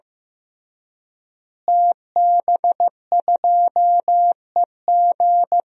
Morse code, heard as a keyed tone, about 16 wpm, 700 Hz. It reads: TB2EG
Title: DXCC-100 Call Signs - 5 Characters - Repeated 3x - 15wpm